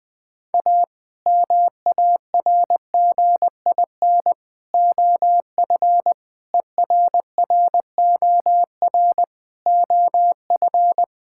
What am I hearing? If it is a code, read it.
Morse code, heard as a keyed tone, about 20 wpm, 700 Hz. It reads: A MARGIN OF ERROR OF
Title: Sets of 8 Words - 20wpm